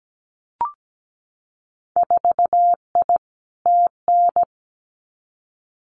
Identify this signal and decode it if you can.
Morse code, heard as a keyed tone, about 17 wpm, 700 Hz. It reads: 4I TN
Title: Winter Field Day Exchanges 17wpm